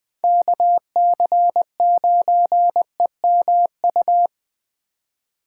Morse code, heard as a keyed tone, about 20 wpm, 700 Hz. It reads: KC9EMU
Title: Speed Racing - USA 2x3 Call Signs 17wpm